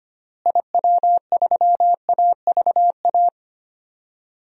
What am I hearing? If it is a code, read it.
Morse code, heard as a keyed tone, about 25 wpm, 700 Hz. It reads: IW3AVA
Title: DXCC-100 Call Signs - 6 Characters - Repeated 2x - 25wpm